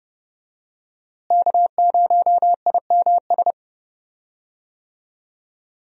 Morse code, heard as a keyed tone, about 32 wpm, 700 Hz. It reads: K0IMS
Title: USA Call Signs - 5 Characters - Repeated 2x - 30wpm